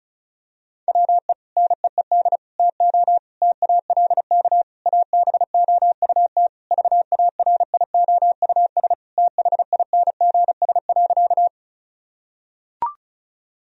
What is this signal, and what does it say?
Morse code, heard as a keyed tone, about 34 wpm, 700 Hz. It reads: WE NEED TO TALK ABOUT VARIOUS THINGS.
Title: Sentences from Top 1000 Words - Review Part 4 - 35wpm